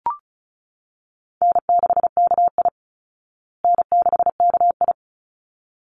Morse code, heard as a keyed tone, about 34 wpm, 700 Hz. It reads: N6KI N6KI
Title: Call Signs Repeated 3x - 35wpm